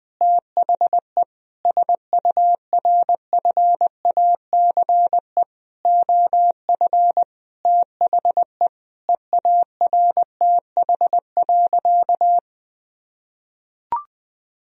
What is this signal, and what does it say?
Morse code, heard as a keyed tone, about 20 wpm, 700 Hz. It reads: THE SURFACE OF THE EARTH.
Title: Sentences from Top 1000 Words - Review Part 3 - 20wpm